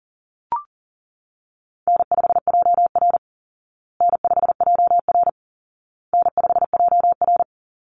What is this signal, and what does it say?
Morse code, heard as a keyed tone, about 40 wpm, 700 Hz. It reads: N5JR N5JR N5JR
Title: Call Signs Repeated 2x - 40wpm